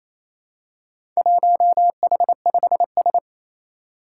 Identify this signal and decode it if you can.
Morse code, heard as a keyed tone, about 27 wpm, 700 Hz. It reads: 1H5S
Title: Four Challenging Letter-Number Combinations 28wpm